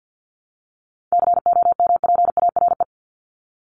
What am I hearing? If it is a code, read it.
Morse code, heard as a keyed tone, about 50 wpm, 700 Hz. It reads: COMPARE
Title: Top 1251-1500 Words 50wpm